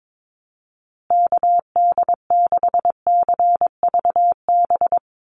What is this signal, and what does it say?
Morse code, heard as a keyed tone, about 22 wpm, 700 Hz. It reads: KD6CVB
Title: USA Call Signs Repeated 3x 22wpm